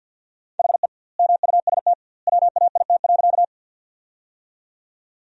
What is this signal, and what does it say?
Morse code, heard as a keyed tone, about 48 wpm, 700 Hz. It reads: HE MUST WAIT.
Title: 2x Word-Spacing - Sentences from Top 500 Words - 50wpm